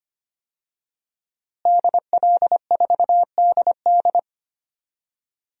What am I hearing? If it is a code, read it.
Morse code, heard as a keyed tone, about 25 wpm, 700 Hz. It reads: DL4DD
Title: DXCC-100 Call Signs - 5 Characters - Repeated 2x - 25wpm